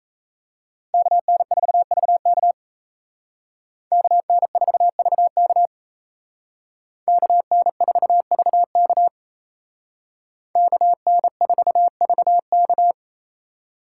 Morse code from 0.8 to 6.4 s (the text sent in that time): KN4VK KN4VK